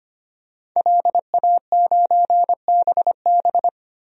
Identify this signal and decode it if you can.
Morse code, heard as a keyed tone, about 25 wpm, 700 Hz. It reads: LA9BB
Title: DXCC-100 Call Signs - 5 Characters - 25wpm